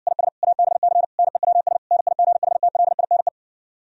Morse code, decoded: IS ABC NEWS NIGHTLINE

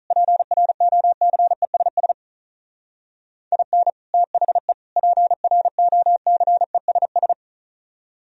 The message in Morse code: PROCESS IN THE PROCESS

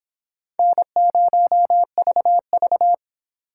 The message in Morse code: N0VV

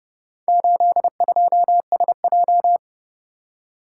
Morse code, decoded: 82SJ

